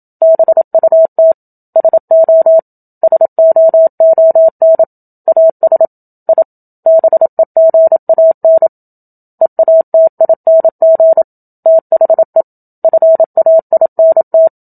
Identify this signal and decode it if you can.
Morse code, heard as a keyed tone, about 27 wpm, 650 Hz. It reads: BUT SO SOON AS I BEGAN EATING THE FAINT
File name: Book Two / 01-Chapter 1 - 27 WPM